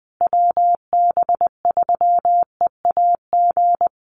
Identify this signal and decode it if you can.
Morse code, heard as a keyed tone, about 20 wpm, 700 Hz. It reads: WB3EAG
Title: USA Call Signs Repeated 3x 20wpm